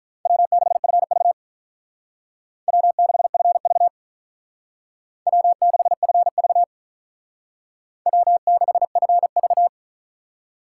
Morse code, heard as a keyed tone, about 42 wpm, 700 Hz. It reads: W6FV W6FV W6FV W6FV
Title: Speed Racing - USA 1x2 Call Signs 35wpm